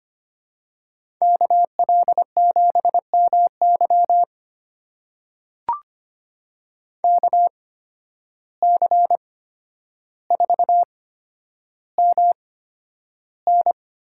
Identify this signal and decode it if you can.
Morse code, heard as a keyed tone, about 25 wpm, spaced as for 11 wpm, 700 Hz. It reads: KL7MY KC4MN
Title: USA Call Signs - 5 Characters - 8x Character Spacing - 25wpm